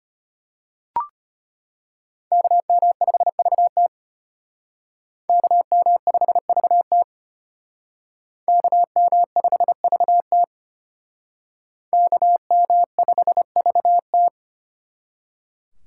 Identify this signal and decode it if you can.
Morse code, heard as a keyed tone, about 32 wpm, 700 Hz. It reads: KM5VT KM5VT KM5VT KM5VT